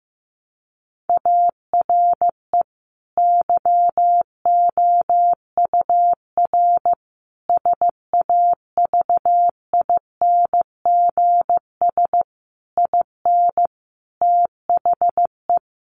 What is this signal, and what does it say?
Morse code, heard as a keyed tone, about 14 wpm, 700 Hz. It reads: ARE YOUR SAVINGS IN THE
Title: Sentences from Top 1200 Words 15wpm